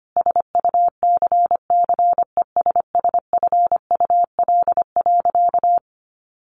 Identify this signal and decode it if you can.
Morse code, heard as a keyed tone, about 25 wpm, 700 Hz. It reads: SUCCESSFUL.
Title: Sentences from Top 1300 Words 25wpm